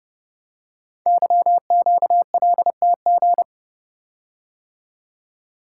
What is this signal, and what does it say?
Morse code, heard as a keyed tone, about 30 wpm, 700 Hz. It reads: YQLTG
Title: Code Groups - 5 Letters - 30wpm